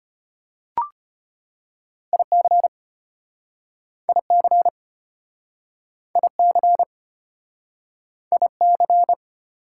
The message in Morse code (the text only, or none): IC IC IC IC